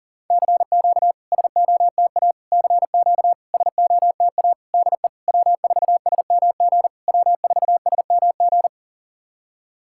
Morse code, heard as a keyed tone, about 40 wpm, 700 Hz. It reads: CQ SOTA CQ SOTA DE W4SMG W4SMG
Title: SOTA QSO Practice 40wpm